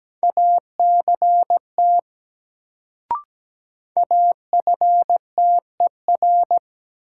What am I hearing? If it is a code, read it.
Morse code, heard as a keyed tone, about 17 wpm, 700 Hz. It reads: ACT AFTER
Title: Sentences from Top 200 Words - New Word Drill - 17wpm